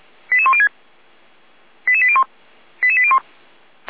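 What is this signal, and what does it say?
Five-tone selective call (ZVEI), heard as quick runs of five tones, tones of about 80 ms. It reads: 88106 88801 88801
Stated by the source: ZVEI five-tone selective call for address 88103, received by SDR at feuerwehr-orlamuende (Orlamünde fire brigade).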